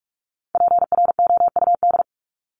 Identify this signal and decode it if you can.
Morse code, heard as a keyed tone, about 44 wpm, 700 Hz. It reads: PROUD
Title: Top 1501-1750 Words 45wpm